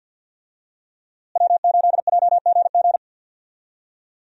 Morse code, heard as a keyed tone, about 53 wpm, 700 Hz. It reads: W8JGG
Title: USA Call Signs - 5 Characters - Repeated 3x - 50wpm